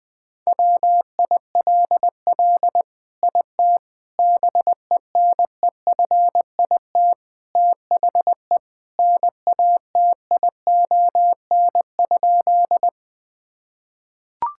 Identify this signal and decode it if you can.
Morse code, heard as a keyed tone, about 20 wpm, 700 Hz. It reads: WILL IT BENEFIT THE NATION?
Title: Sentences from Top 1000 Words - Review Part 6 - 20wpm